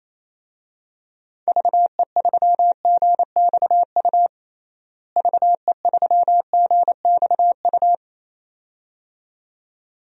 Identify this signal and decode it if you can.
Morse code, heard as a keyed tone, about 28 wpm, 700 Hz. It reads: VE3GXU VE3GXU
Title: DXCC-100 Call Signs - 6 Characters - Repeated 2x - 28wpm